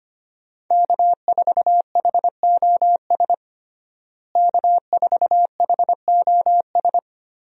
Morse code, heard as a keyed tone, about 25 wpm, 700 Hz. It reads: K4HOS K4HOS
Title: USA Call Signs - 5 Characters - Repeated 3x - 25wpm